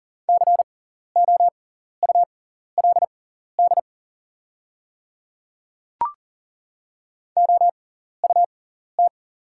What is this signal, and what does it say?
Morse code, heard as a keyed tone, about 40 wpm, spaced as for 14 wpm, 700 Hz. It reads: COULD OUT
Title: 6x Character Spacing - Top 200 Words - 40wpm